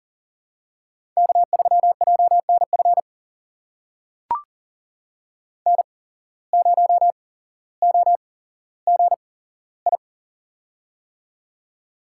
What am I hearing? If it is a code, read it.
Morse code, heard as a keyed tone, about 43 wpm, spaced as for 17 wpm, 700 Hz. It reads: K3JNF N0OGI
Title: USA Call Signs - 5 Characters - 8x Character Spacing - 40wpm